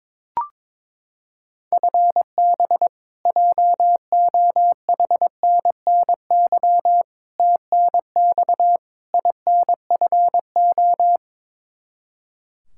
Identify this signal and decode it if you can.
Morse code, heard as a keyed tone, about 22 wpm, 700 Hz. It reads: FB JOHNNY TNX INFO
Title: Practice QSO - Exchange 3 - More Info - 22wpm